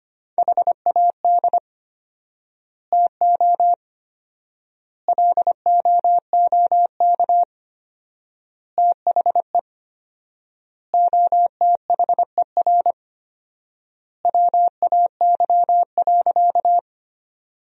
Morse code, heard as a keyed tone, about 25 wpm, 700 Hz. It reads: HAD TO LOOK THE OTHER WAY.
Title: 4x Word-Spacing - Sentences from Top 100 Words - 25wpm